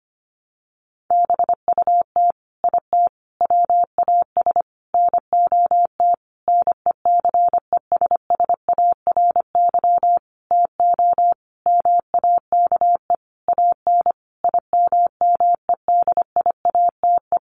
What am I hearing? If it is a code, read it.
Morse code, heard as a keyed tone, about 25 wpm, 700 Hz. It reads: BUT IT WAS NOT NECESSARY TO MAKE AN IMMEDIATE
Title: Sentences from Top 1200 Words 25wpm